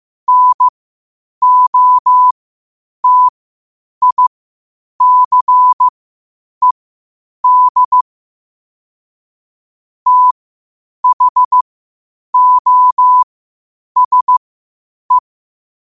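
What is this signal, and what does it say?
Morse code, heard as a keyed tone, about 15 wpm, spaced as for 8 wpm, 1000 Hz. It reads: NOTICED THOSE
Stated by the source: Morse code quotes at 8 words per minute from QOTD at 8 WPM.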